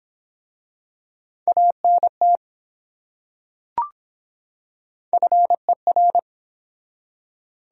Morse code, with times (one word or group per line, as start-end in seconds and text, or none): 1.5-2.4 ANT
5.1-6.2 FER